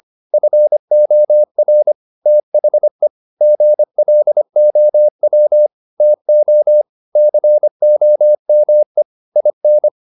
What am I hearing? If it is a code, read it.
Morse code, heard as a keyed tone, about 25 wpm, 600 Hz. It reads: FOR THE GLOW TO COME IN